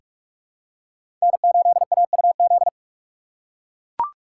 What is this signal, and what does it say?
Morse code, heard as a keyed tone, about 43 wpm, 700 Hz. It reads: N8AUZ
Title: USA Call Signs - 5 Characters - 4x Character Spacing - 45wpm